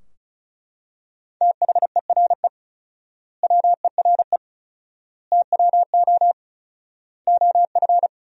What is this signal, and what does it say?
Morse code, heard as a keyed tone, about 36 wpm, 700 Hz. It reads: THERE WERE TWO OF
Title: 4x Word-Spacing - Sentences from Top 100 Words - 35wpm